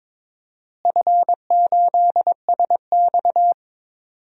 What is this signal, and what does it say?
Morse code, heard as a keyed tone, about 22 wpm, 700 Hz. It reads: F8SX
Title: DXCC-100 Call Signs - 4 Characters - Repeated 2x - 22wpm